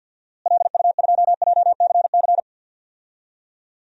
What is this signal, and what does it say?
Morse code, heard as a keyed tone, about 52 wpm, 700 Hz. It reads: LU2JCC
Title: DXCC-100 Call Signs - 6 Characters - Repeated 2x - 50wpm